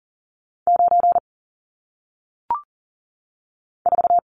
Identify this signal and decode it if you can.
Morse code, heard as a keyed tone, about 39 wpm, 700 Hz. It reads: 9 4
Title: Numbers - 1-Digit 40wpm